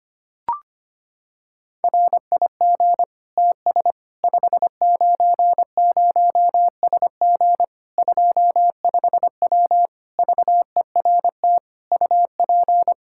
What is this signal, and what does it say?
Morse code, heard as a keyed tone, about 25 wpm, 700 Hz. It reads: RIG TS 590SG 25W VERT UP
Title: Practice QSO - Exchange 2 - Info - 25wpm